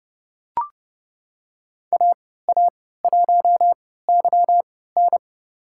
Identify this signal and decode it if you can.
Morse code, heard as a keyed tone, about 30 wpm, spaced as for 17 wpm, 700 Hz. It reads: AA1YN
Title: USA Call Signs - 5 Characters - 3x Character Spacing - 30wpm